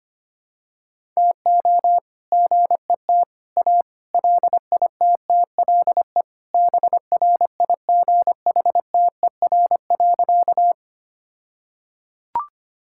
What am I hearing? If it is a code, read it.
Morse code, heard as a keyed tone, about 25 wpm, 700 Hz. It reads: TO GET A LITTLE BRIGHTER.
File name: practice / 25wpm / All-Age Appropriate Jokes - 25wpm